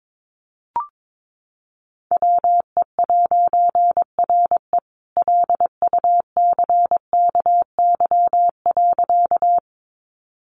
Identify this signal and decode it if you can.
Morse code, heard as a keyed tone, about 22 wpm, 700 Hz. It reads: WE'RE LUCKY.
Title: A Sound Of Thunder 22wpm